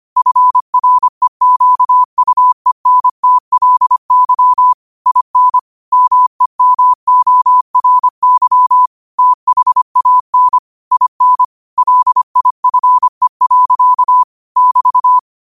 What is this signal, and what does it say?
Morse code, heard as a keyed tone, about 25 wpm, 1000 Hz. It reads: FREQUENTLY IN MEMORY THAN IN LIFE. <BT>
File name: qotdfiles / 25wpm